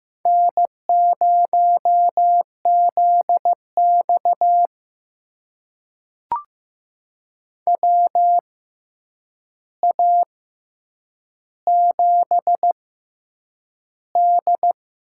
Morse code, heard as a keyed tone, about 15 wpm, spaced as for 8 wpm, 700 Hz. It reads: N0ZX WA7D
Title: USA Call Signs - 4 Characters - 6x Character Spacing - 15wpm